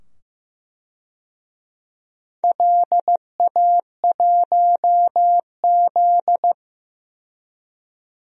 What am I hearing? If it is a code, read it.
Morse code, heard as a keyed tone, about 15 wpm, 700 Hz. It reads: LA1Z
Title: DXCC-100 Call Signs - 4 Characters - 15wpm